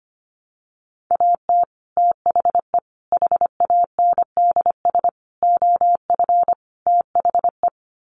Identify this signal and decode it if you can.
Morse code, heard as a keyed tone, about 25 wpm, 700 Hz. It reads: AT THE HANDS OF THE
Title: Sets of 5 Words Encore2 25wpm